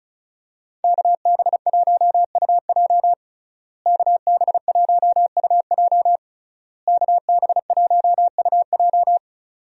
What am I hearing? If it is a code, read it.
Morse code, heard as a keyed tone, about 34 wpm, 700 Hz. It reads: KB1UJ KB1UJ KB1UJ
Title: USA Call Signs - 5 Characters - Repeated 3x - 35wpm